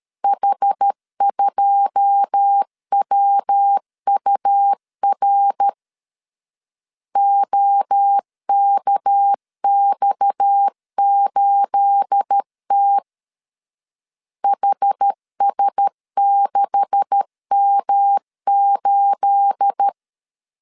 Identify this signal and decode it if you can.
Morse code, heard as a keyed tone, about 13 wpm, 800 Hz. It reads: H2WUR OKX8T HS6M8
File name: CW / DET / 1 9 1500C S C 12wpm 3811